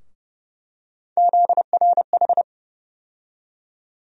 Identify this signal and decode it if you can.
Morse code, heard as a keyed tone, about 32 wpm, 700 Hz. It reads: ZRH
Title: USA Call Sign Postfixes - 3 Letters 30wpm